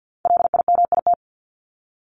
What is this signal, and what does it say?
Morse code, heard as a keyed tone, about 50 wpm, 700 Hz. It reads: LIMIT
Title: Top 1251-1500 Words 50wpm